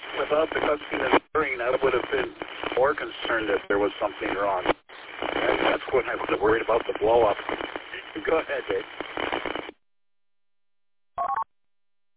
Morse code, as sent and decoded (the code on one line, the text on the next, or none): none